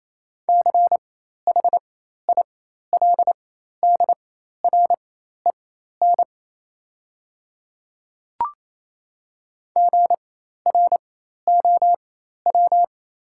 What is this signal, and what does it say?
Morse code, heard as a keyed tone, about 28 wpm, spaced as for 13 wpm, 690 Hz. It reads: CHILDREN GROW
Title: 4x Character Spacing - Top 500 Words - 28wpm